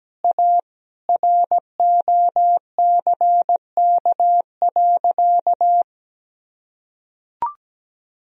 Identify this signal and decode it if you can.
Morse code, heard as a keyed tone, about 17 wpm, 700 Hz. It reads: A ROCK.
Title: Sentences from Top 1000 Words - Review Part 2 - 17wpm